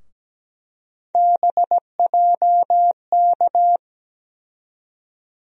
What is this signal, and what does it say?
Morse code, heard as a keyed tone, about 17 wpm, 700 Hz. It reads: BJK
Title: USA Call Sign Postfixes - 3 Letters 17wpm